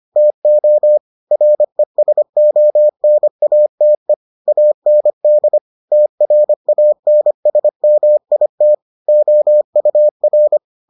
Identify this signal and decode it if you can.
Morse code, heard as a keyed tone, about 25 wpm, 600 Hz. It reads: TO RESONATE AND TRANSMIT OUR